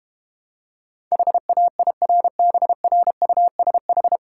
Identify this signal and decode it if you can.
Morse code, heard as a keyed tone, about 32 wpm, 700 Hz. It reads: HAIRBRUSH